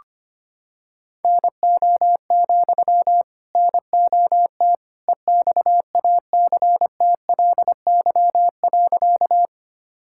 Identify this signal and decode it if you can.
Morse code, heard as a keyed tone, about 25 wpm, 700 Hz. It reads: NO, NOT EXACTLY.
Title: Sentences from Top 1000 Words - Review Part 6 - 25wpm